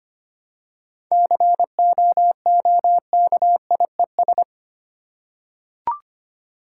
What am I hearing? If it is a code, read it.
Morse code, heard as a keyed tone, about 25 wpm, 700 Hz. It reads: COOKIES
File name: practice / 25wpm / Easy Words with Hints - 25wpm